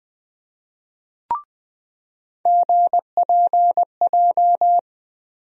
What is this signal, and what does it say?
Morse code, heard as a keyed tone, about 20 wpm, 700 Hz. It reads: GPJ
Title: USA Call Sign Postfixes - 3 Letters 20wpm